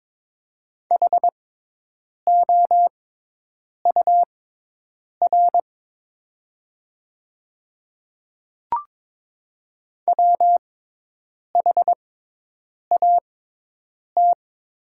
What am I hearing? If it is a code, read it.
Morse code, heard as a keyed tone, about 22 wpm, spaced as for 8 wpm, 700 Hz. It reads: HOUR WHAT
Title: 6x Character Spacing - Top 500 Words - 22wpm